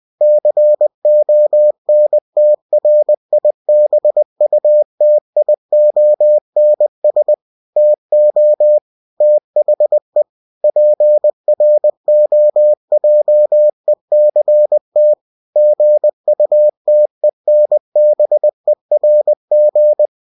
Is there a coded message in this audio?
Yes: Morse code from 0.2 to 20.1 s, heard as a keyed tone, about 20 wpm, 600 Hz. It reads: CONTRIBUTIONS TO THE PROJECT GUTENBERG